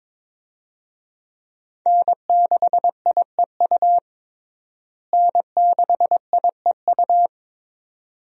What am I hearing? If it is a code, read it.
Morse code, heard as a keyed tone, about 22 wpm, 700 Hz. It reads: N6IEU N6IEU
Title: USA Call Signs - 5 Characters - Repeated 2x - 22wpm